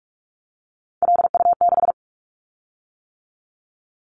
Morse code, heard as a keyed tone, about 46 wpm, 700 Hz. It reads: LU6
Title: Callsign Prefixes for Top 50 DXCC - 45wpm